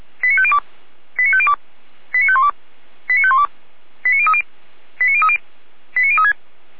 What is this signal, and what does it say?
Five-tone selective call (ZVEI), heard as quick runs of five tones, tones of about 70 ms. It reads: 89502 89502 89512 89512 89030 89030 89036
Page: /455/SDR/ZVEI/